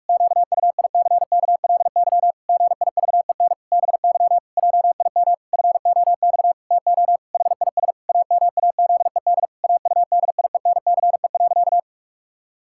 Morse code, heard as a keyed tone, about 41 wpm, 700 Hz. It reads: QUICKLY GIVEN BY JIM FOX TO HIS AMAZED AUDIENCE.
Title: Pangram Challenge v2 - 45wpm